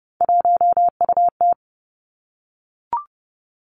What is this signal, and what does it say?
Morse code, heard as a keyed tone, about 30 wpm, 700 Hz. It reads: 1UT
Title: Three Challenging Letter-Number Combinations 30wpm